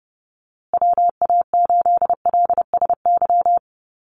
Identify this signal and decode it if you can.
Morse code, heard as a keyed tone, about 30 wpm, 700 Hz. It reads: WA8LSY
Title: USA Call Signs Repeated 3x 30wpm